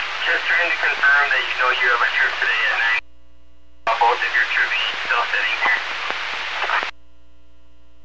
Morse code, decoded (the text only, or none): none